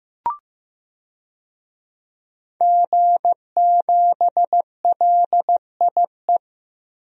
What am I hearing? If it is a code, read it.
Morse code, heard as a keyed tone, about 15 wpm, 700 Hz. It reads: G7LIE
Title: DXCC-100 Call Signs - 5 Characters - Repeated 2x - 15wpm